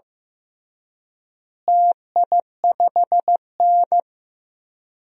Morse code, heard as a keyed tone, about 15 wpm, 700 Hz. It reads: TI5N
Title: DXCC-100 Call Signs - 4 Characters - Repeated 3x - 15wpm